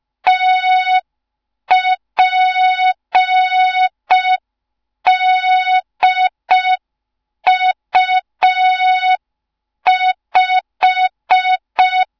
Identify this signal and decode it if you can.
Morse code, heard as a keyed tone, about 5 wpm, 3700 Hz. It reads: TPDU5
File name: Morze-HA2MN / hang / 07 GYAKORLAT